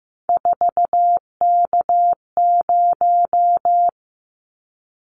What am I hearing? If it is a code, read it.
Morse code, heard as a keyed tone, about 15 wpm, 700 Hz. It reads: 4K0
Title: Callsign Prefixes for Top 100 DXCC - 15wpm